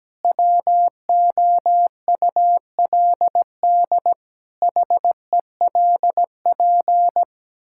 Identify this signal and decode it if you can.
Morse code, heard as a keyed tone, about 17 wpm, 700 Hz. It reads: WOULD HELP
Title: Sentences from Top 1000 Words - Review Part 5 - 17wpm